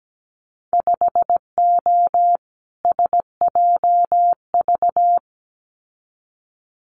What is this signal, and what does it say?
Morse code, heard as a keyed tone, about 17 wpm, 700 Hz. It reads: 5O SJV
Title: Winter Field Day Exchanges Encore 17wpm